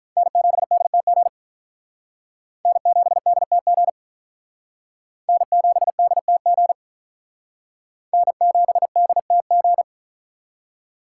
Morse code, heard as a keyed tone, about 45 wpm, 700 Hz. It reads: N7DTG N7DTG N7DTG N7DTG